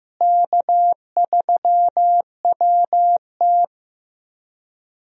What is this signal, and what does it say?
Morse code, heard as a keyed tone, about 15 wpm, 700 Hz. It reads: K3WT
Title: USA Call Signs - 4 Characters - Repeated 3x - 15wpm